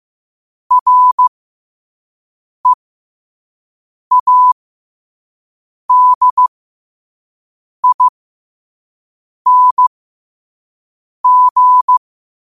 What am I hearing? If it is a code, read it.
Morse code, heard as a keyed tone, about 15 wpm, spaced as for 5 wpm, 1000 Hz. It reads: READING
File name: qotdfiles / 5wpm